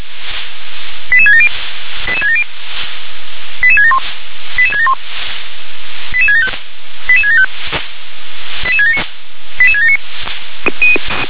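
Five-tone selective call (ZVEI), heard as quick runs of five tones, tones of about 70 ms. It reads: 88670 88670 88671 88671 88675 88675 88679 88679
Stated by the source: ZVEI five-tone selective call 88776 received by SDR.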